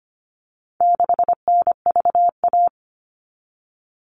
Morse code, heard as a keyed tone, about 25 wpm, 700 Hz. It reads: 6NVA
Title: All Cut Numbers - 4-Digit 25wpm